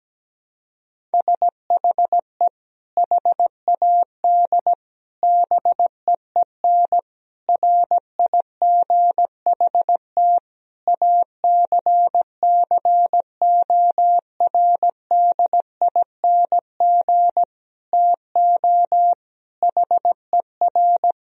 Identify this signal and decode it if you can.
Morse code, heard as a keyed tone, about 17 wpm, 700 Hz. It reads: SHE HAD BEEN RIGHT ACCORDING TO HER
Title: Sentences from Top 1000 Words - Review Part 4 - 17wpm